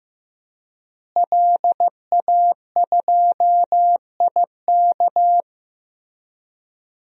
Morse code, heard as a keyed tone, about 15 wpm, 700 Hz. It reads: LA2IK
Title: DXCC-100 Call Signs - 5 Characters - Repeated 2x - 15wpm